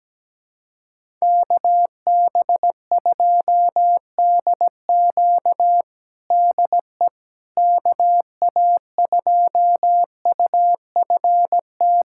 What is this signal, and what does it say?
Morse code, heard as a keyed tone, about 17 wpm, 700 Hz. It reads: KB2DQ DE KA2UFT
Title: USA Callsign DE Callsign 17wpm